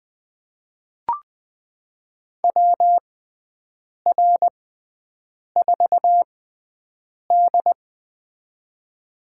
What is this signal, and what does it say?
Morse code, heard as a keyed tone, about 20 wpm, spaced as for 7 wpm, 700 Hz. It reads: WR4D
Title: USA Call Signs - 6x Character Spacing - 20wpm